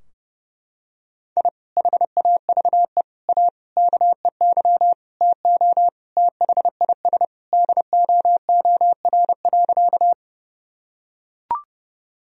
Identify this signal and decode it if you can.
Morse code, heard as a keyed tone, about 30 wpm, 700 Hz. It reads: I HAVE A KEY TO THIS DOOR.
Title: Sentences from Top 1000 Words - Review Part 4 - 30wpm